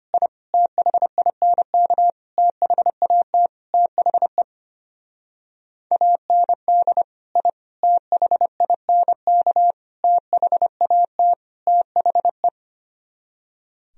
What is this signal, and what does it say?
Morse code, heard as a keyed tone, about 31 wpm, 700 Hz. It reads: I THINK THAT THE AND I THINK THAT THE
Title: Speed-Racing - Sets of 5 Words 25wpm